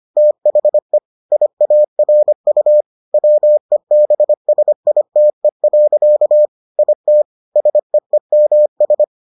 Morse code, heard as a keyed tone, about 25 wpm, 600 Hz. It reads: THE IARU WEBSITE. IT SEEMS